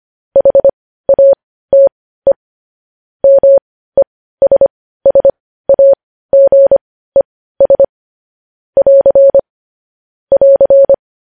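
Morse code, heard as a keyed tone, about 25 wpm, spaced as for 15 wpm, 550 Hz. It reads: HATE MESSAGES <AR> <AR>
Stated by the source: Morse code news headlines sent at 15.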